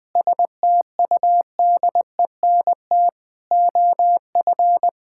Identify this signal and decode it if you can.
Morse code, heard as a keyed tone, about 21 wpm, 700 Hz. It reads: STUDENT OF
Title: Sentences from Top 1000 Words - Review Part 6 - 20wpm